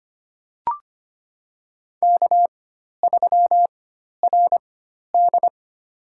Morse code, heard as a keyed tone, about 25 wpm, spaced as for 12 wpm, 700 Hz. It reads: K3RD
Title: USA Call Signs - 4 Characters - 4x Character Spacing - 25wpm